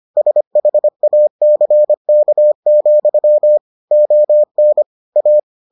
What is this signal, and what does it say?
Morse code, heard as a keyed tone, about 25 wpm, 600 Hz. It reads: SHACK, ON A